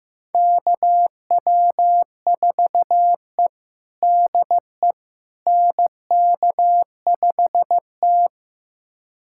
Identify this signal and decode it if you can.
Morse code, heard as a keyed tone, about 16 wpm, 700 Hz. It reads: KW4E DE NK5T
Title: USA Callsign DE Callsign - 4 Characters - 15wpm